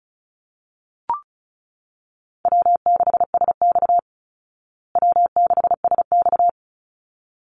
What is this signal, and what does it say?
Morse code, heard as a keyed tone, about 34 wpm, 700 Hz. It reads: W6SX W6SX
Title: Call Signs Repeated 3x - 35wpm